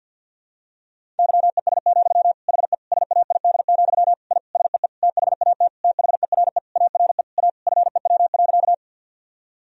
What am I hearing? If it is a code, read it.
Morse code, heard as a keyed tone, about 46 wpm, 700 Hz. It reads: YES, HE SAID, I SEE THAT THERE ARE A FEW.
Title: Sentences from Top 1000 Words - Review Part 1 - 50wpm